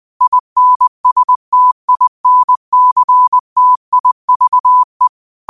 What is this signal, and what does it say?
Morse code, heard as a keyed tone, about 20 wpm, 1000 Hz. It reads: INSTINCTIVE